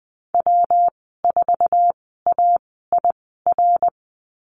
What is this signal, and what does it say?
Morse code, heard as a keyed tone, about 19 wpm, spaced as for 14 wpm, 700 Hz. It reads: W4AIR
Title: 147 Lesson 24g - Letter y - Call Signs 20wpm